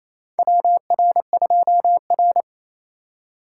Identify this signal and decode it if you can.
Morse code, heard as a keyed tone, about 28 wpm, 700 Hz. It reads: WR2R
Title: USA Call Signs - 4 Characters - 4x Character Spacing - 28wpm